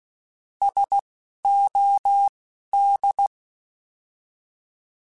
Morse code, heard as a keyed tone, about 16 wpm, spaced as for 11 wpm, 800 Hz. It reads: SOD
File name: Alphabetical / Alphabetical mp3s / Ss @10